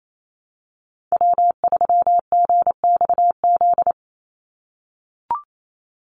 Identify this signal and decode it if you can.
Morse code, heard as a keyed tone, about 28 wpm, 700 Hz. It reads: W3GXZ
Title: USA Call Signs 28wpm